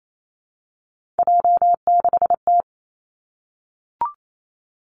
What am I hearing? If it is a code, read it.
Morse code, heard as a keyed tone, about 28 wpm, 700 Hz. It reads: J6T
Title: Three Challenging Letter-Number Combinations 28wpm